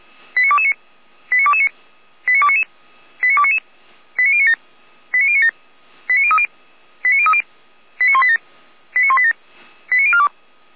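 Five-tone selective call (ZVEI), heard as quick runs of five tones, tones of about 80 ms. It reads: 89229 89229 89299 89299 89097 89097 89030 89030 89187 89187 89042
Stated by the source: ZVEI five-tone call received by SDR, Feuerwehr Orlamünde.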